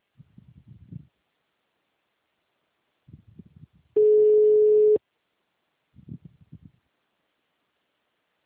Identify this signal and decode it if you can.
Ringback tone (Europe, 425 Hz).
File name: out- -2011- - - .11888